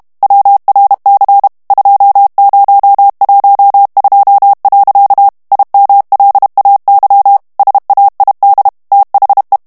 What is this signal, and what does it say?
Morse code, heard as a keyed tone, about 32 wpm, 800 Hz. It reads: WRC 2012. IMLAY SAID THE
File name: morse mp3 / ANGLAIS texte clair 3 vitesse 30 mots minute